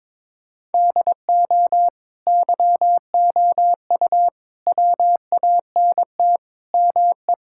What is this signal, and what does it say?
Morse code, heard as a keyed tone, about 22 wpm, 700 Hz. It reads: DO YOU WANT ME
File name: practice / 22wpm / Sentences with 4-Word Phrases - 22wpm